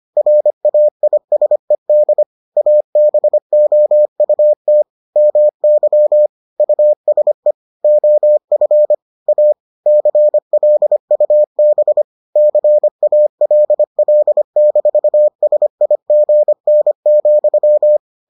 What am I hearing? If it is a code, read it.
Morse code, heard as a keyed tone, about 25 wpm, 600 Hz. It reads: RAISED ABOUT MY USE OF A CLUB CALL-SIGN,